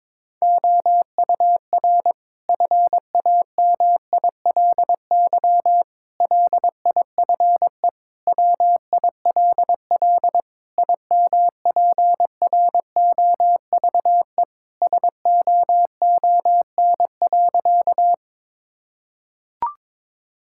Morse code, heard as a keyed tone, about 22 wpm, 700 Hz. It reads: OUR FAMILY LIFE WILL IMPROVE SOON.
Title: Sentences from Top 1000 Words - Review Part 5 - 22wpm